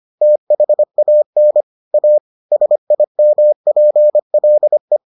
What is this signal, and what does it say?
Morse code, heard as a keyed tone, about 25 wpm, 600 Hz. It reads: THAN A SIMPLE